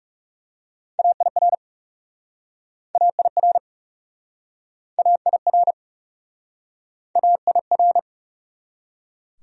Morse code, heard as a keyed tone, about 36 wpm, 700 Hz. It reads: AIR AIR AIR AIR
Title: Speed-Racing Top 200 Words 30wpm